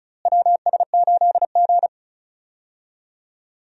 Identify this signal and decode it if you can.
Morse code, heard as a keyed tone, about 34 wpm, 700 Hz. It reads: WS8G
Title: USA Call Signs - 4 Characters - Repeated 3x - 35wpm